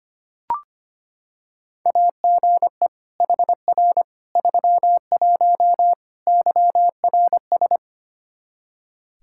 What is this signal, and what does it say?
Morse code, heard as a keyed tone, about 25 wpm, 700 Hz. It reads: AGE HR 31 YRS
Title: Practice QSO - Exchange 3 - More Info - 25wpm